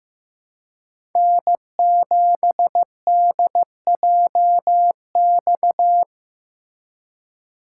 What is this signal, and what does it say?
Morse code, heard as a keyed tone, about 15 wpm, 700 Hz. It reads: N7DJX
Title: USA Call Signs - 5 Characters - 2x Character Spacing - 15wpm